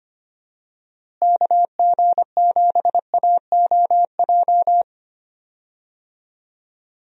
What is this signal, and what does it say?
Morse code, heard as a keyed tone, about 25 wpm, 700 Hz. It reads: KG7AOJ
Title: USA Call Signs - 6 Characters - Repeated 3x - 25wpm